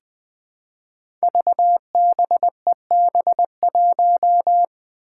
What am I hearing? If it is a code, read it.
Morse code, heard as a keyed tone, about 20 wpm, 700 Hz. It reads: VBEB1
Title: Five Challenging Letter-Number Combinations 20wpm